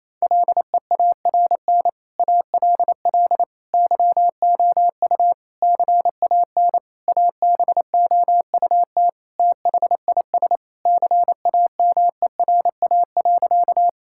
Morse code, heard as a keyed tone, about 28 wpm, 700 Hz. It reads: LEARN ALL YOU CAN ABOUT THIS CAMERA.
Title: Sentences from Top 1000 Words - Review Part 5 - 28wpm